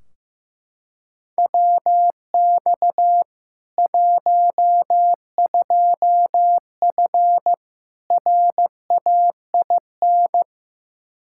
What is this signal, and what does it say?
Morse code, heard as a keyed tone, about 15 wpm, 700 Hz. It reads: WX 12F RAIN
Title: K1USN SST QSO Practice 15wpm